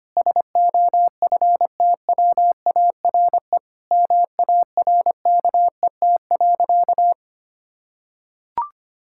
Morse code, heard as a keyed tone, about 25 wpm, 700 Hz. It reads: SOFTWARE MARKET.
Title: Sentences from Top 1000 Words - Review Part 6 - 25wpm